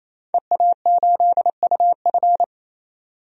Morse code, heard as a keyed tone, about 28 wpm, 700 Hz. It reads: EA8UF
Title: DXCC-100 Call Signs - 5 Characters - 28wpm